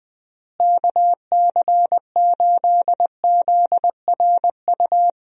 Morse code, heard as a keyed tone, about 21 wpm, 700 Hz. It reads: KC8ZRU
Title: USA Call Signs - 6 Characters - Repeated 3x - 20wpm